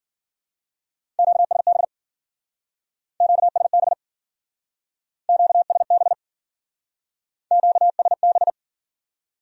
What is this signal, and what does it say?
Morse code, heard as a keyed tone, about 50 wpm, 700 Hz. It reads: QSB QSB QSB QSB
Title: Speed-Racing Q-Codes - 40wpm